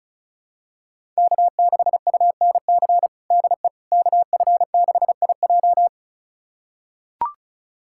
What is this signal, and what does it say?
Morse code, heard as a keyed tone, about 34 wpm, 700 Hz. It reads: K6UNC DE KF6IJ
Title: USA Callsign DE Callsign - 5 Characters - 35wpm